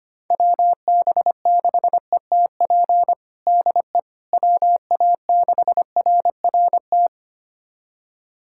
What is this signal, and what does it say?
Morse code, heard as a keyed tone, about 25 wpm, 700 Hz. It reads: WB6ETP DE WA6RRT
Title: USA Callsign DE Callsign - 6 Characters - 25wpm